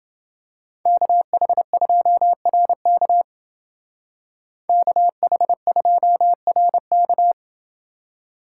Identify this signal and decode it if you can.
Morse code, heard as a keyed tone, about 29 wpm, 700 Hz. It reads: KH2RK KH2RK